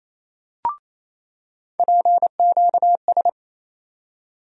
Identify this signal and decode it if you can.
Morse code, heard as a keyed tone, about 28 wpm, 700 Hz. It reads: PQS
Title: USA Call Sign Postfixes - 3 Letters 28wpm